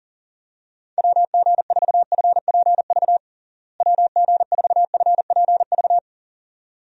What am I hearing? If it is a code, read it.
Morse code, heard as a keyed tone, about 37 wpm, 700 Hz. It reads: WG4FPV WG4FPV
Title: USA Call Signs - 6 Characters - Repeated 2x - 40wpm